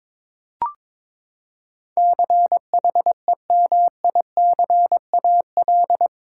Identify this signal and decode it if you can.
Morse code, heard as a keyed tone, about 22 wpm, 700 Hz. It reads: CHEMICAL
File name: practice / 22wpm / Eight Letter Words - 22wpm